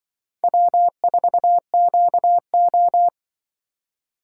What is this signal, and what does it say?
Morse code, heard as a keyed tone, about 24 wpm, 700 Hz. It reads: W4QO
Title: Speed Racing - USA 1x2 Call Signs 20wpm